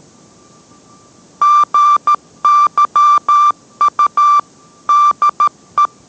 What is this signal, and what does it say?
Morse code, heard as a keyed tone, about 13 wpm, 1180 Hz. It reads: GYU DE